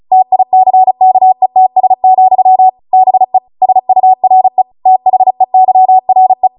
Morse code, heard as a keyed tone, about 36 wpm, 750 Hz. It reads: TICKETS, BE SURE THEYRE
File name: Morse / 35WPM